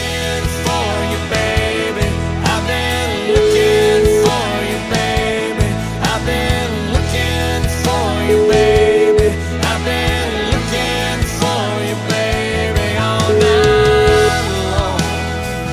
Ringback tone (Europe, 425 Hz), with noise.